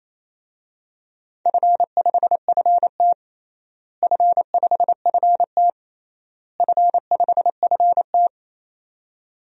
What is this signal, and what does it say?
Morse code, heard as a keyed tone, about 28 wpm, 700 Hz. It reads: F5FT F5FT F5FT
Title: DXCC-100 Call Signs - 4 Characters - Repeated 3x - 28wpm